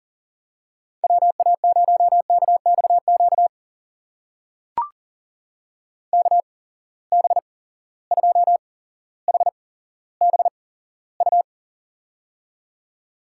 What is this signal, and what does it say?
Morse code, heard as a keyed tone, about 40 wpm, spaced as for 17 wpm, 700 Hz. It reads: WA0KXQ KB2HBU
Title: USA Call Signs - 6 Characters - 8x Character Spacing - 40wpm